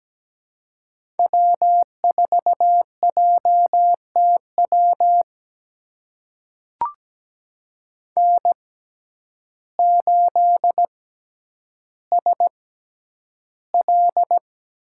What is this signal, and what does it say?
Morse code, heard as a keyed tone, about 17 wpm, spaced as for 9 wpm, 700 Hz. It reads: W4JTW N8SL
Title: USA Call Signs - 5 Characters - 6x Character Spacing - 17wpm